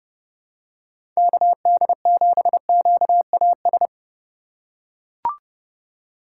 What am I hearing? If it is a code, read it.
Morse code, heard as a keyed tone, about 30 wpm, 700 Hz. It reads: KD7QAS
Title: USA Call Signs - 6 Characters - 2x Character Spacing - 30wpm